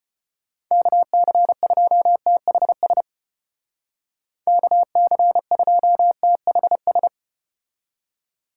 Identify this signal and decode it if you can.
Morse code, heard as a keyed tone, about 33 wpm, 700 Hz. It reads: KC2THS KC2THS